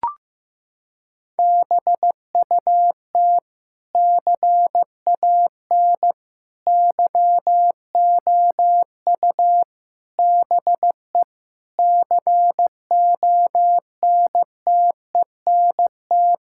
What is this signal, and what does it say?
Morse code, heard as a keyed tone, about 15 wpm, 700 Hz. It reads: BUT CAN YOU BE CONTENT
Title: Sentences from Top 1000 Words - Review Part 6 - 15wpm